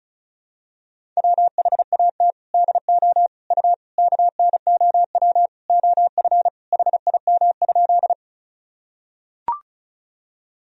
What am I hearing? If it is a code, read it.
Morse code, heard as a keyed tone, about 36 wpm, 700 Hz. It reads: WHAT DO U KNOW OF HIM?
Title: Sentences from Top 100 Words and Abbreviations 35wpm